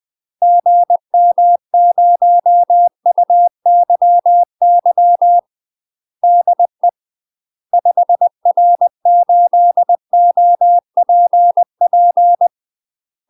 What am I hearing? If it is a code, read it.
Morse code, heard as a keyed tone, about 20 wpm, 700 Hz. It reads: GM0UYY DE 5R8OPP
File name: UK MASTER QSOs TRAINING / 20WPM / UK MASTER QSOs 0072